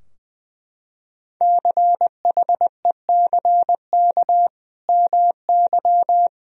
Morse code, heard as a keyed tone, about 20 wpm, 700 Hz. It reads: CHECK MY
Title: Sentences from Top 1000 Words - Review Part 3 - 20wpm